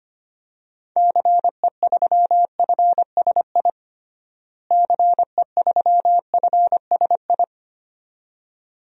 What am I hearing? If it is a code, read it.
Morse code, heard as a keyed tone, about 25 wpm, 700 Hz. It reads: CE3FSI CE3FSI
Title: DXCC-100 Call Signs - 6 Characters - Repeated 2x - 25wpm